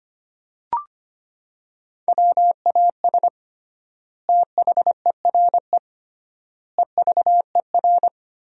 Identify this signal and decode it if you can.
Morse code, heard as a keyed tone, about 25 wpm, 700 Hz. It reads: WAS THERE EVER
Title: 3x Word-Spacing - Sentences from Top 400 Words - 25wpm